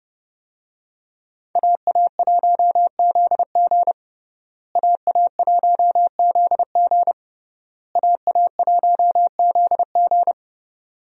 Morse code, heard as a keyed tone, about 29 wpm, 700 Hz. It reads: AA1ZG AA1ZG AA1ZG
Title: USA Call Signs - 5 Characters - Repeated 3x - 30wpm